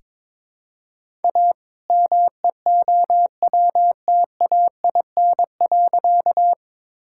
Morse code, heard as a keyed tone, about 22 wpm, 700 Hz. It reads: A MEOWTAIN.
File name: All-Age Appropriate Jokes - 22wpm